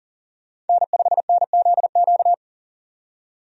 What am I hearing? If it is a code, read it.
Morse code, heard as a keyed tone, about 40 wpm, 700 Hz. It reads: N5NZQ
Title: USA Call Signs - 5 Characters - 40wpm